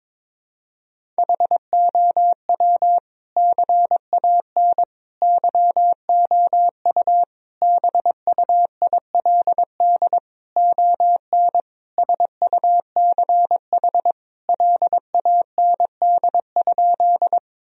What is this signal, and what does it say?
Morse code, heard as a keyed tone, about 22 wpm, 700 Hz. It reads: HOW CAN YOU BUILD ON SUCH LAND?
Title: Sentences from Top 1000 Words - Review Part 1 - 22wpm